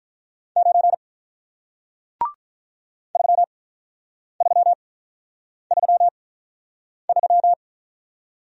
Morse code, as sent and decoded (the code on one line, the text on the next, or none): ----. / ...-- ...-- ...-- ...--
9 3333